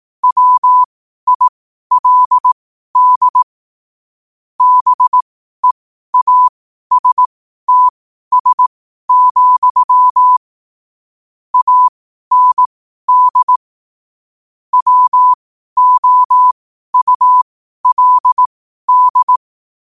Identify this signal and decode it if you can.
Morse code, heard as a keyed tone, about 18 wpm, spaced as for 12 wpm, 1000 Hz. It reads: WILD BEASTS, AND WOULD
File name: qotdfiles / 12wpm